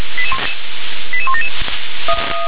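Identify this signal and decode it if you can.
Five-tone selective call (ZVEI), heard as quick runs of five tones, tones of about 70 ms. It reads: 88177
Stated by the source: ZVEI five-tone selective call received by SDR.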